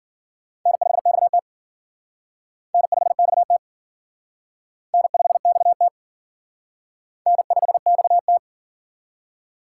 Morse code, heard as a keyed tone, about 50 wpm, 700 Hz. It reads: N5XT N5XT N5XT N5XT
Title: Speed Racing - USA 1x2 Call Signs 40wpm